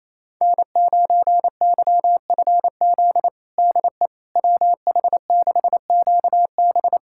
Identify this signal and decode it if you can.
Morse code, heard as a keyed tone, about 28 wpm, 700 Hz. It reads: N9YFZ DE WH6QB
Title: USA Callsign DE Callsign - 5 Characters - 28wpm